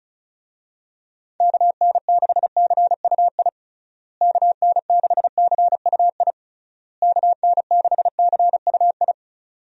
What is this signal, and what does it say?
Morse code, heard as a keyed tone, about 36 wpm, 700 Hz. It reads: KN6CUI KN6CUI KN6CUI
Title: USA Call Signs - 6 Characters - Repeated 3x - 35wpm